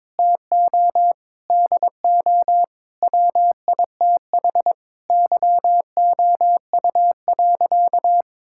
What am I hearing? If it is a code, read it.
Morse code, heard as a keyed tone, about 22 wpm, 700 Hz. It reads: TO DO WITH YOU.
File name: practice / 22wpm / Sentences with 4-Word Phrases - 22wpm